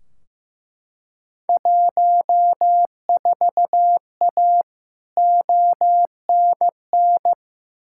Morse code, heard as a keyed tone, about 15 wpm, 700 Hz. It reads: 14A ONN
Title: Field Day Exchanges Encore - 15wpm